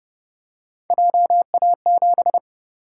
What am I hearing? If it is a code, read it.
Morse code, heard as a keyed tone, about 30 wpm, 700 Hz. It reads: JA7
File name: Speed-Racing - Three Letter-Number Combinations - 25wpm